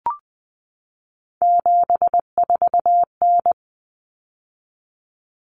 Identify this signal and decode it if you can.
Morse code, heard as a keyed tone, about 20 wpm, 700 Hz. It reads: 74N
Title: Common Cut Numbers - 3-Digit 20wpm